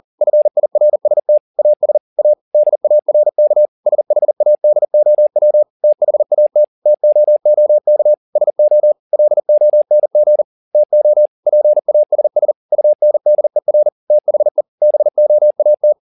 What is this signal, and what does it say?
Morse code, heard as a keyed tone, about 40 wpm, 600 Hz. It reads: FIRST AS A DARK SHADOW THAT TOOK SO LONG TO PASS UNDER THE BOAT
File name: the old man and the sea 2 40wpm